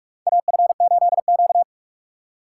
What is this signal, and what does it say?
Morse code, heard as a keyed tone, about 46 wpm, 700 Hz. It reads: AF8Q
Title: USA Call Signs - 2x Character Spacing - 45wpm